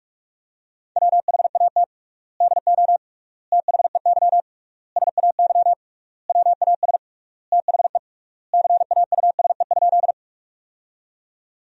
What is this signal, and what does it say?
Morse code, heard as a keyed tone, about 46 wpm, 700 Hz. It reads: WHAT DO THEY SAY WAS THE CAUSE?
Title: 3x Word-Spacing - Sentences from Top 200 Words - 45wpm